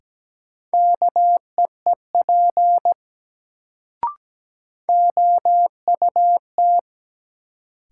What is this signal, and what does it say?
Morse code, heard as a keyed tone, about 17 wpm, 700 Hz. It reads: KEEP OUT
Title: Sets of 2 Words - New Word Drill - 17wpm